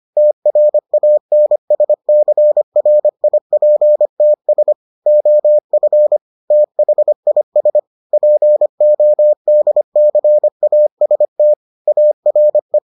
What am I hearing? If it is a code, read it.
Morse code, heard as a keyed tone, about 25 wpm, 600 Hz. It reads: TRANSCRIPTS OF THIS PODCAST ARE